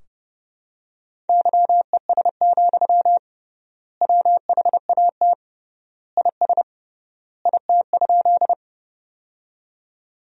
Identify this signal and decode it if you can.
Morse code, heard as a keyed tone, about 30 wpm, 700 Hz. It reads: YES, WHAT IS IT?
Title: 3x Word-Spacing - Sentences from Top 500 Words - 30wpm